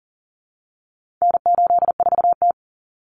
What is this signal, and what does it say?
Morse code, heard as a keyed tone, about 40 wpm, 700 Hz. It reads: N84T
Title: Common Cut Numbers - 4-Digit 40wpm